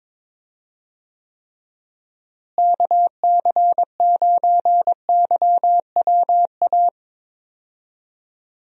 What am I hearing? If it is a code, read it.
Morse code, heard as a keyed tone, about 22 wpm, 700 Hz. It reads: KC9YWA
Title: USA Call Signs - 6 Characters - Repeated 2x - 22wpm